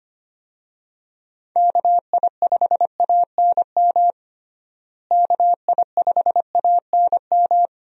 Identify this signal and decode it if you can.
Morse code, heard as a keyed tone, about 25 wpm, 700 Hz. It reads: KI5ANM KI5ANM
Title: USA Call Signs - 6 Characters - Repeated 3x - 25wpm